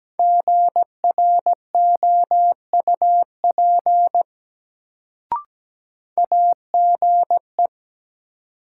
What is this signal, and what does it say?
Morse code, heard as a keyed tone, about 17 wpm, 700 Hz. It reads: GROUP AGE
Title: Sets of 3 Words - New Word Drill - 17wpm